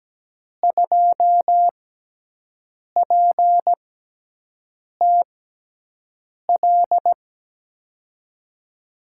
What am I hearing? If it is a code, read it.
Morse code, heard as a keyed tone, about 17 wpm, 700 Hz. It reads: 2 P T L
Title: USA Call Signs - 5 Characters - 6x Character Spacing - 17wpm